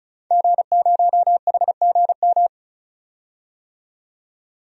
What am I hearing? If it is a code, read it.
Morse code, heard as a keyed tone, about 36 wpm, 700 Hz. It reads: G0HGM
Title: DXCC-100 Call Signs - 5 Characters - 35wpm